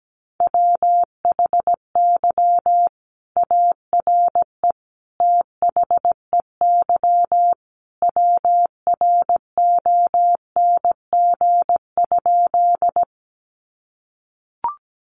Morse code, heard as a keyed tone, about 17 wpm, 700 Hz. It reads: WHY ARE THEY WRONG?
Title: Sentences from Top 900 Words 17wpm